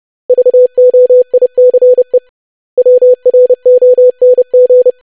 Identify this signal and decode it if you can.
Morse code, heard as a keyed tone, about 30 wpm, 500 Hz. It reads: VOICE WRONG
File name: lesson / web / basic500-25wpm